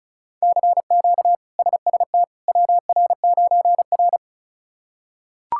Morse code, heard as a keyed tone, about 34 wpm, 700 Hz. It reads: CQ SST WR9R
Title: K1USN SST QSO Practice - 35wpm